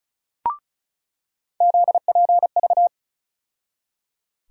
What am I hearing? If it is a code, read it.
Morse code, heard as a keyed tone, about 34 wpm, 700 Hz. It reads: ZPV
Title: USA Call Sign Postfixes - 3 Letters 35wpm